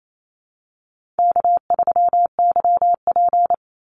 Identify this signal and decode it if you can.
Morse code, heard as a keyed tone, about 28 wpm, 700 Hz. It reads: K3YP
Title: Easy Call Signs - 28wpm